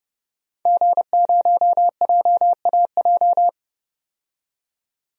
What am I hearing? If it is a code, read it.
Morse code, heard as a keyed tone, about 29 wpm, 700 Hz. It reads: G0JAJ
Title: DXCC-100 Call Signs - 5 Characters - 30wpm